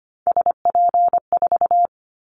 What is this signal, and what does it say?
Morse code, heard as a keyed tone, about 25 wpm, 700 Hz. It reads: SP4
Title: Callsign Prefixes for Top 25 DXCC - 25wpm